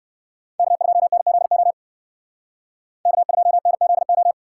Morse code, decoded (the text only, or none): K2NZO K2NZO